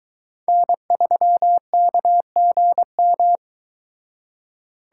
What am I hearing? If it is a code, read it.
Morse code, heard as a keyed tone, about 23 wpm, 700 Hz. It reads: N3KGM